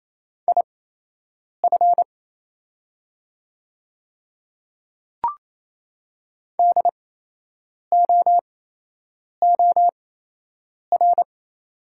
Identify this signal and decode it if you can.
Morse code, heard as a keyed tone, about 28 wpm, spaced as for 8 wpm, 700 Hz. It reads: IF DOOR
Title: 8x Character Spacing - Top 500 Words - 28wpm